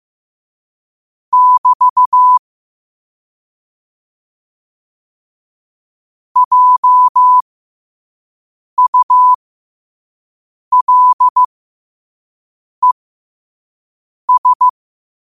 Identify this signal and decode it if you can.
Morse code, heard as a keyed tone, about 15 wpm, spaced as for 5 wpm, 1000 Hz. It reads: <BT> JULES
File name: qotdfiles / 5wpm